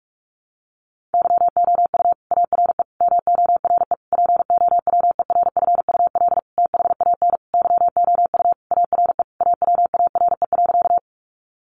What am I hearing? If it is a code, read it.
Morse code, heard as a keyed tone, about 44 wpm, 700 Hz. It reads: YOU ARE MORE POWERFUL THAN YOU ARE AWARE.
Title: Sentences from Top 1300 Words 45wpm